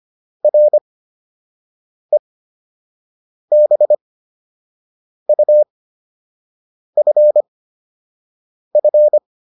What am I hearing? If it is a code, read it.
Morse code, heard as a keyed tone, about 25 wpm, spaced as for 6 wpm, 600 Hz. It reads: REBUFF